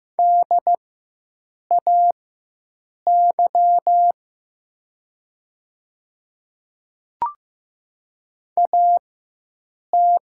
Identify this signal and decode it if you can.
Morse code, heard as a keyed tone, about 15 wpm, spaced as for 7 wpm, 700 Hz. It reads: DAY AT
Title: 4x Character Spacing - Top 100 Words - 15wpm